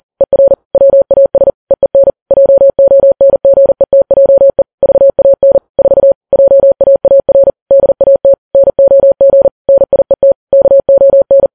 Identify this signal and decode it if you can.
Morse code, heard as a keyed tone, about 40 wpm, 550 Hz. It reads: ER WAS EEN JONGETJE VAN 4 JAAR DAT NOG NIET KON